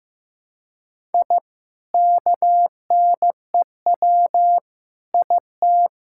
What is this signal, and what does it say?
Morse code, heard as a keyed tone, about 15 wpm, 700 Hz. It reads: I KNEW IT
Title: Sentences from Top 1000 Words - Review Part 5 - 15wpm